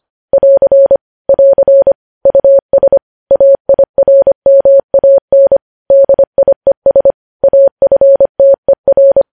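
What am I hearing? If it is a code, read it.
Morse code, heard as a keyed tone, about 25 wpm, 550 Hz. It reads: <AR> <AR> US AIRMAN DIES AFTER
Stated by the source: Morse code news headlines at 25, opening with vvv.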